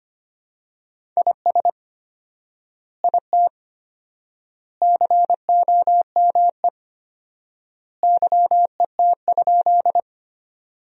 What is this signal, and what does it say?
Morse code, heard as a keyed tone, about 25 wpm, 700 Hz. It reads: IS IT COME YET?
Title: 4x Word-Spacing - Sentences from Top 500 Words - 25wpm